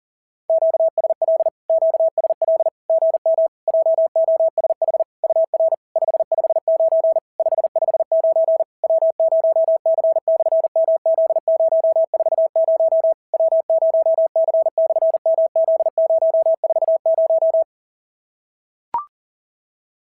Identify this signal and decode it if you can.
Morse code, heard as a keyed tone, about 40 wpm, 690 Hz. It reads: QSL QSL GM JOSH UR 559 559 W0C/MZ040 W0C/MZ040
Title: SOTA QSO Practice 40wpm